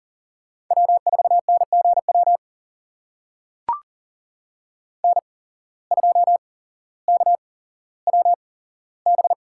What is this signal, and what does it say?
Morse code, heard as a keyed tone, about 40 wpm, spaced as for 17 wpm, 700 Hz. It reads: W4NGW N2KWB
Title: USA Call Signs - 5 Characters - 8x Character Spacing - 40wpm